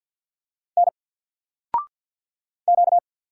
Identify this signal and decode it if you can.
Morse code, heard as a keyed tone, about 48 wpm, 700 Hz. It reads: N Q
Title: Single Character (Letter:Number:Punctuation:Prosign) 50wpm